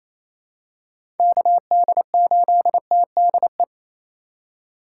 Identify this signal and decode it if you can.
Morse code, heard as a keyed tone, about 28 wpm, 700 Hz. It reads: KD8TDE
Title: USA Call Signs - 6 Characters - 2x Character Spacing - 28wpm